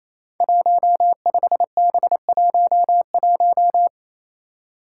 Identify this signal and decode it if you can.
Morse code, heard as a keyed tone, about 28 wpm, 700 Hz. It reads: 15B11
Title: Five Challenging Letter-Number Combinations 28wpm